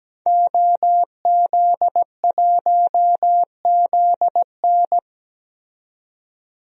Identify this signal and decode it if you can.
Morse code, heard as a keyed tone, about 17 wpm, 700 Hz. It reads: OZ1ZN
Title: DXCC-100 Call Signs - 5 Characters - 17wpm